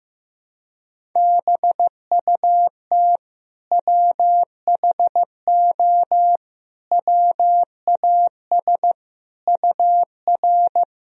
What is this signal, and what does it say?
Morse code, heard as a keyed tone, about 16 wpm, 700 Hz. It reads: BUT WHO WAS UR
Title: Sentences from Top 200 Words and Abbreviations 15wpm